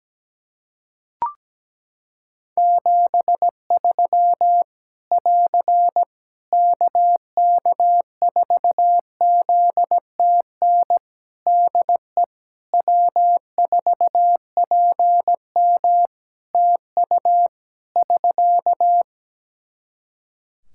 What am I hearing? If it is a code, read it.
Morse code, heard as a keyed tone, about 17 wpm, 700 Hz. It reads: 73 <AR> KK4ZTN DE W4PM TU <SK>
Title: Practice QSO - Exchange 4 - Ending - 17wpm